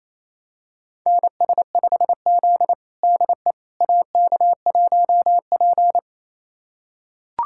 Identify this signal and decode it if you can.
Morse code, heard as a keyed tone, about 28 wpm, 700 Hz. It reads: NS5Z DE AK1P
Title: USA Callsign DE Callsign - 4 Characters - 28wpm